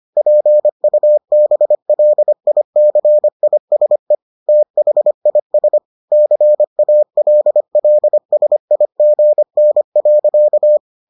Morse code, heard as a keyed tone, about 25 wpm, 600 Hz. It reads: PUBLICISE THIS CALLSIGN.